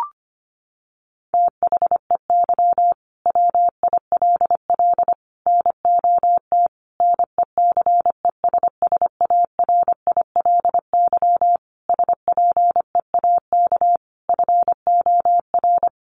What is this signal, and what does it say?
Morse code, heard as a keyed tone, about 25 wpm, 700 Hz. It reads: THEY WILL NOT NECESSARILY SPEAK FOR
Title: Sentences from Top 1400 Words 25wpm